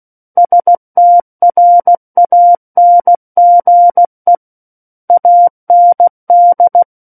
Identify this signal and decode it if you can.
Morse code, heard as a keyed tone, about 16 wpm, 700 Hz. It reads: STRANGE AND
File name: Book One / 07-Chapter 7 - 16 WPM